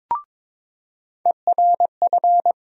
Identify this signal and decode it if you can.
Morse code, heard as a keyed tone, about 22 wpm, 700 Hz. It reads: ERF
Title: USA Call Sign Postfixes - 3 Letters 22wpm